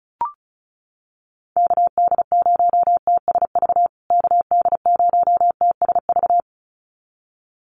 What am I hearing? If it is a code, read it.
Morse code, heard as a keyed tone, about 34 wpm, 700 Hz. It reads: KD0TSV KD0TSV
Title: USA Call Signs Repeated 2x 35wpm